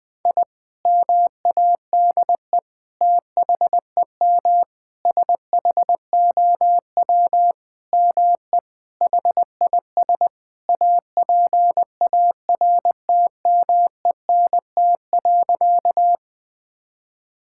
Morse code, heard as a keyed tone, about 20 wpm, 700 Hz. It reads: I MADE THEM SHOW ME HIS APARTMENT.
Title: Sentences from Top 1000 Words - Review Part 6 - 20wpm